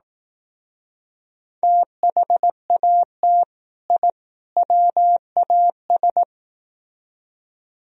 Morse code, heard as a keyed tone, about 18 wpm, 700 Hz. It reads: THAT I WAS
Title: Speed-Racing - Sets of 3 Words 15wpm